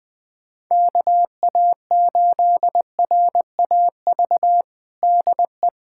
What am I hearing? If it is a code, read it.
Morse code, heard as a keyed tone, about 21 wpm, 700 Hz. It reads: KA8RAV DE
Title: USA Callsign DE Callsign - 6 Characters - 20wpm